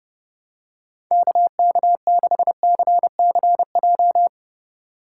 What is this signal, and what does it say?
Morse code, heard as a keyed tone, about 29 wpm, 700 Hz. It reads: KK6CCJ
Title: USA Call Signs - 6 Characters - 2x Character Spacing - 30wpm